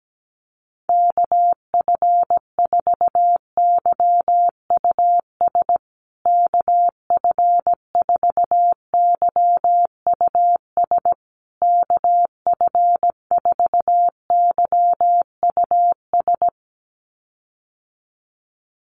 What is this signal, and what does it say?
Morse code, heard as a keyed tone, about 17 wpm, 700 Hz. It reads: KF4YUS KF4YUS KF4YUS
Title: USA Call Signs Repeated 3x 17wpm